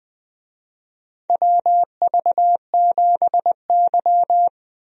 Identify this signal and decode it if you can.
Morse code, heard as a keyed tone, about 20 wpm, 700 Hz. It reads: WV7Y
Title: USA Call Signs - 4 Characters - 20wpm